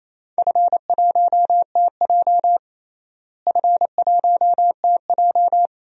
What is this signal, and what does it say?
Morse code, heard as a keyed tone, about 28 wpm, 700 Hz. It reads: F1TJ F1TJ
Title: DXCC-100 Call Signs - 4 Characters - Repeated 2x - 28wpm